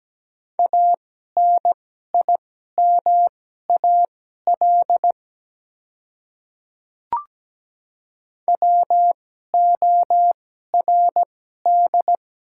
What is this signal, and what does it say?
Morse code, heard as a keyed tone, about 17 wpm, spaced as for 12 wpm, 700 Hz. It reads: ANIMAL WORD
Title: 2x Character Spacing - Top 200 Words - 17wpm